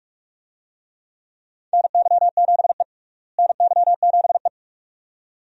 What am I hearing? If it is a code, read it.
Morse code, heard as a keyed tone, about 44 wpm, 700 Hz. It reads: NY7E NY7E
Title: USA Call Signs - 4 Characters - Repeated 2x - 45wpm